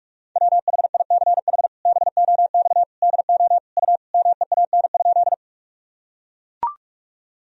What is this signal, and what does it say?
Morse code, heard as a keyed tone, about 43 wpm, 700 Hz. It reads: WHICH BOX DO U MEAN?
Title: Sentences from Top 500 Words and Abbreviations 45wpm